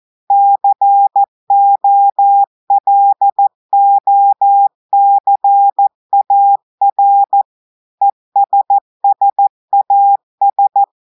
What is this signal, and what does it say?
Morse code, heard as a keyed tone, about 14 wpm, 800 Hz. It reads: COLOCAR ESSAS
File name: audios / portugais / portugais texte en clair 2 vitesse 14 mots minutes